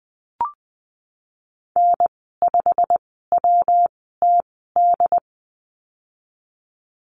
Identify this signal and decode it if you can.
Morse code, heard as a keyed tone, about 20 wpm, spaced as for 15 wpm, 700 Hz. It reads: N5WTD
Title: 154 Lesson 25g - Letter p - Call Signs 20wpm